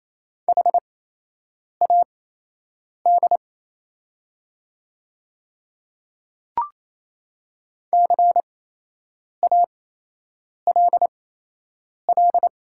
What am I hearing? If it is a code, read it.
Morse code, heard as a keyed tone, about 28 wpm, spaced as for 8 wpm, 700 Hz. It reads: HAD CALL
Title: 8x Character Spacing - Top 100 Words - 28wpm